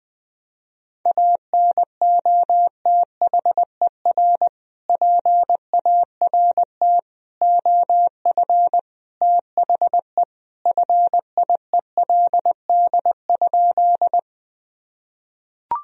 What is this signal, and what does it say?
Morse code, heard as a keyed tone, about 20 wpm, 700 Hz. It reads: ANOTHER PART OF THE FIELD?
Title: Sentences from Top 1000 Words - Review Part 2 - 20wpm